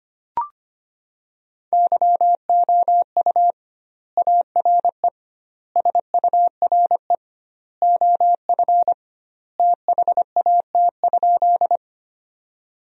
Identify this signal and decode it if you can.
Morse code, heard as a keyed tone, about 25 wpm, 700 Hz. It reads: YOU ARE SURE OF THAT?
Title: 2x Word-Spacing - Sentences from Top 300 Words - 25wpm